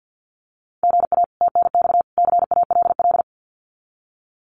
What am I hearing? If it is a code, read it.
Morse code, heard as a keyed tone, about 50 wpm, 700 Hz. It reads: GA TNX CALL
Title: Common QSO Phrases 50wpm